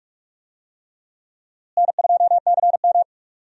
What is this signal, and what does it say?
Morse code, heard as a keyed tone, about 42 wpm, 700 Hz. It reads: N2CM
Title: USA Call Signs - 4 Characters - Repeated 3x - 45wpm